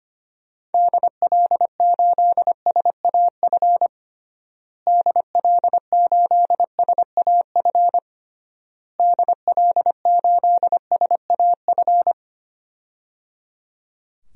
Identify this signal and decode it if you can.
Morse code, heard as a keyed tone, about 25 wpm, 700 Hz. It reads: DL8SAF DL8SAF DL8SAF
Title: DXCC-100 Call Signs - 6 Characters - Repeated 3x - 25wpm